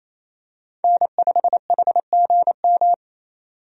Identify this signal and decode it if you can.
Morse code, heard as a keyed tone, about 28 wpm, 700 Hz. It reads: N5HGM
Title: USA Call Signs - 5 Characters - 28wpm